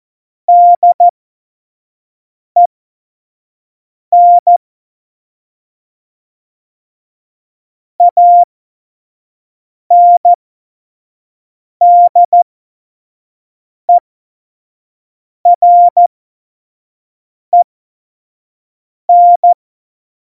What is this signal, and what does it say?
Morse code, heard as a keyed tone, about 14 wpm, spaced as for 5 wpm, 700 Hz. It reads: DEN ANDEREN
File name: Morsekurs / Witze / Witz 22. Mai 2018, 25 F70